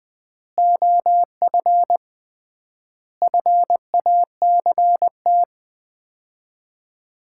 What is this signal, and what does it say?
Morse code, heard as a keyed tone, about 19 wpm, 700 Hz. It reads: OF FACT
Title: 3x Word-Spacing - Sets of 5 Words - 20wpm